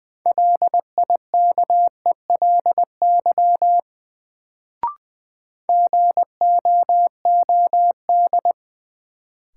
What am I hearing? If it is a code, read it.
Morse code, heard as a keyed tone, about 20 wpm, 700 Hz. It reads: LIKELY GOOD
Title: Sets of 2 Words - New Word Drill - 20wpm